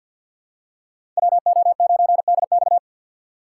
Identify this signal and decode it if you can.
Morse code, heard as a keyed tone, about 47 wpm, 700 Hz. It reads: WO9DX
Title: USA Call Signs - 5 Characters - 2x Character Spacing - 50wpm